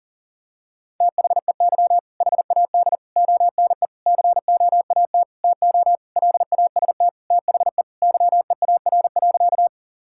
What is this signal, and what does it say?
Morse code, heard as a keyed tone, about 40 wpm, 700 Hz. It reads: THEY HAD ONE COAT TO LAST THE YEAR.
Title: Sentences from Top 1600 Words 40wpm